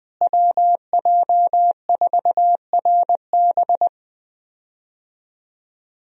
Morse code, heard as a keyed tone, about 20 wpm, 700 Hz. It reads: WJ4RB
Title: USA Call Signs - 5 Characters - Repeated 3x - 20wpm